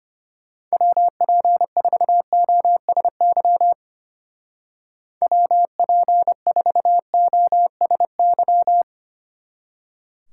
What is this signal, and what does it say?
Morse code, heard as a keyed tone, about 32 wpm, 700 Hz. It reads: WP4OSY WP4OSY